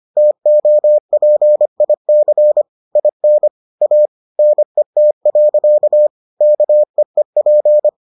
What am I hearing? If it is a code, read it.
Morse code, heard as a keyed tone, about 25 wpm, 600 Hz. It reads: TOPIC IN A NET. KEEP